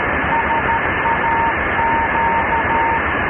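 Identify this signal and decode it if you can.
Morse code, heard as a keyed tone, about 13 wpm, 920 Hz, with noise. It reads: SAQ
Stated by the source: SAQ Grimeton Christmas transmission, received in Potsdam about 527 km away on a Grundig Satellit 700 at LSB 4017.6 kHz via converter, with ferrite and degaussing-coil loop antennas.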